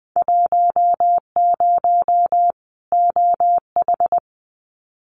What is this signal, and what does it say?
Morse code, heard as a keyed tone, about 20 wpm, 700 Hz. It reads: 10 OH
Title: Winter Field Day Exchanges 20wpm